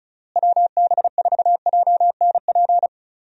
Morse code, heard as a keyed tone, about 34 wpm, 700 Hz. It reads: WB4JNP
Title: USA Call Signs - 6 Characters - Repeated 3x - 35wpm